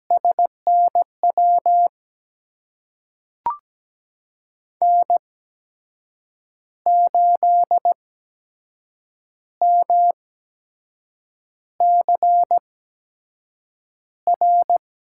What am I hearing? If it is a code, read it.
Morse code, heard as a keyed tone, about 17 wpm, spaced as for 6 wpm, 700 Hz. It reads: SNW N8MCR
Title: USA Call Signs - 5 Characters - 8x Character Spacing - 17wpm